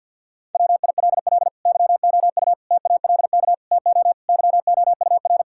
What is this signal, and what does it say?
Morse code, heard as a keyed tone, about 52 wpm, 700 Hz. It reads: WILL YOU TALK TO YOUR
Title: Sentences from Top 1600 Words 50wpm